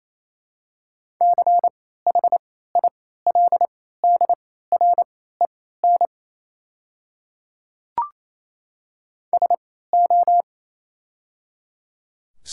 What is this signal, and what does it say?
Morse code, heard as a keyed tone, about 27 wpm, spaced as for 16 wpm, 700 Hz. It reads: CHILDREN SO
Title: 3x Character Spacing - Top 500 Words - 28wpm